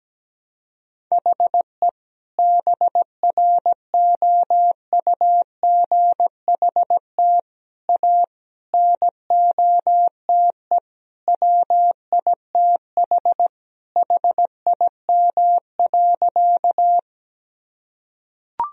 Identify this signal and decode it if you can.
Morse code, heard as a keyed tone, about 17 wpm, 700 Hz. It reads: HE BROUGHT A NOTE WITH HIM.
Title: Sentences from Top 1000 Words - Review Part 2 - 17wpm